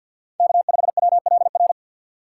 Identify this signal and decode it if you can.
Morse code, heard as a keyed tone, about 53 wpm, 700 Hz. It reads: K5WLR
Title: USA Call Signs - 5 Characters - 2x Character Spacing - 50wpm